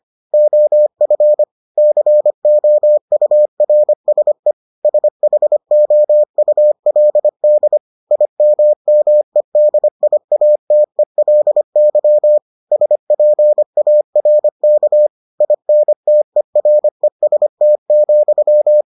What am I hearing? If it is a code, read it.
Morse code, heard as a keyed tone, about 25 wpm, 600 Hz. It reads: OF COURSE SHOULD IMMEDIATELY SPARK INTEREST,